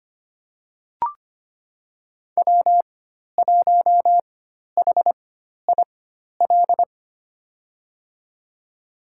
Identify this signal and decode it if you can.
Morse code, heard as a keyed tone, about 25 wpm, spaced as for 12 wpm, 700 Hz. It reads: W1HIL
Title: USA Call Signs - 5 Characters - 4x Character Spacing - 25wpm